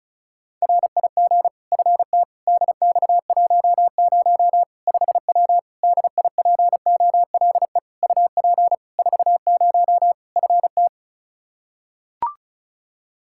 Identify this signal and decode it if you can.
Morse code, heard as a keyed tone, about 34 wpm, 700 Hz. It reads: RIG FT DX10 5W DIPOLE UP 40 FT
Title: Practice QSO - Exchange 2 - Info - 35wpm